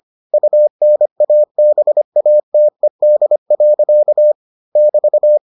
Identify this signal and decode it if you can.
Morse code, heard as a keyed tone, about 25 wpm, 600 Hz. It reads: UNABATED. <BT>